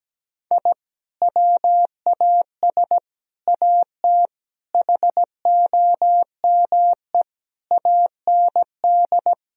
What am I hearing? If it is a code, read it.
Morse code, heard as a keyed tone, about 17 wpm, 700 Hz. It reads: I WAS AT HOME AND
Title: Sentences from Top 1000 Words - Review Part 1 - 17wpm